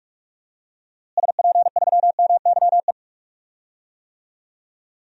Morse code, heard as a keyed tone, about 46 wpm, 700 Hz. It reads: SP3MYE
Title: DXCC-100 Call Signs - 6 Characters - Repeated 3x - 45wpm